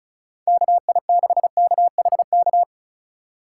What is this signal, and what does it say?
Morse code, heard as a keyed tone, about 34 wpm, 700 Hz. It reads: KI6KHK
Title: USA Call Signs - 6 Characters - 2x Character Spacing - 35wpm